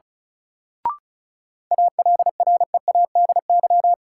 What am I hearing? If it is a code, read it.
Morse code, heard as a keyed tone, about 36 wpm, 700 Hz. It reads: ALREADY
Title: Sentences from Top 700 Words - New Word Drill - 35wpm